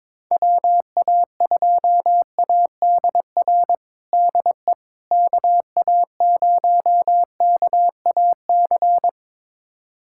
Morse code, heard as a keyed tone, about 22 wpm, 700 Hz. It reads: WA2ADR DE KA0KAC
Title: USA Callsign DE Callsign - 6 Characters - 22wpm